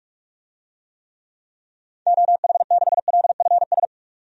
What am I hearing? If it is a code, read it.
Morse code, heard as a keyed tone, about 46 wpm, 700 Hz. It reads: OH6LFS
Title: DXCC-100 Call Signs - 6 Characters - 45wpm